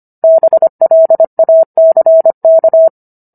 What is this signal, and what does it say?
Morse code, heard as a keyed tone, about 25 wpm, 650 Hz. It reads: BLACK